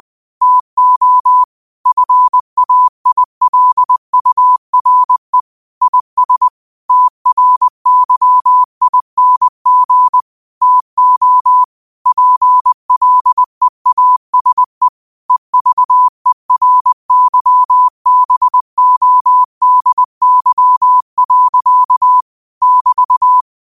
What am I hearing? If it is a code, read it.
Morse code, heard as a keyed tone, about 20 wpm, 1000 Hz. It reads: TO FAILURE IS TRYING TO PLEASE EVERYBODY. <BT>
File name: qotdfiles / 20wpm